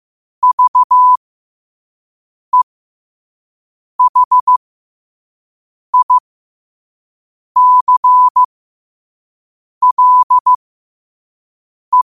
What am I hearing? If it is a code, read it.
Morse code, heard as a keyed tone, about 15 wpm, spaced as for 5 wpm, 1000 Hz. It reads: VEHICLE
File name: qotdfiles / 5wpm